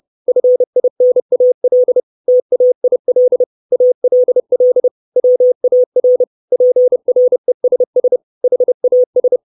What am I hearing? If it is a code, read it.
Morse code, heard as a keyed tone, about 30 wpm, 500 Hz. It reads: FINAL TAIL ALL WAR PRESS HAS
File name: 632 Words 30WPM